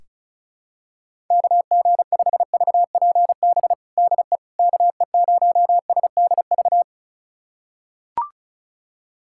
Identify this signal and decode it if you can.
Morse code, heard as a keyed tone, about 34 wpm, 700 Hz. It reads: KG5VPB DE KE0SDV
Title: USA Callsign DE Callsign 35wpm